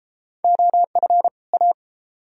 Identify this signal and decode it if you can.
Morse code, heard as a keyed tone, about 32 wpm, 700 Hz. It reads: OF A